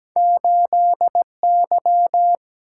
Morse code, heard as a keyed tone, about 17 wpm, 700 Hz. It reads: 8 Y
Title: Single-Letter Single-Number 17wpm